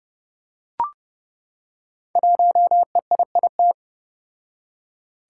Morse code, heard as a keyed tone, about 30 wpm, 700 Hz. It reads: 1EIIT
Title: Five Challenging Letter-Number Combinations 30wpm